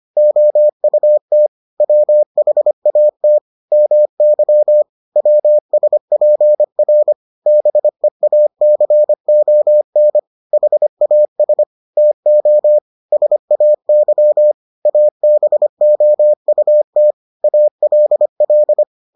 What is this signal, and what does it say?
Morse code, heard as a keyed tone, about 25 wpm, 600 Hz. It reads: OUT WHAT MY WSPR BEACON HAS TO SAY ABOUT ALL